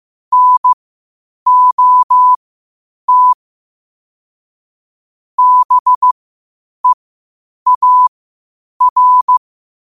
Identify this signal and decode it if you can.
Morse code, heard as a keyed tone, about 15 wpm, spaced as for 8 wpm, 1000 Hz. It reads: NOT BEAR
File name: qotdfiles / 8wpm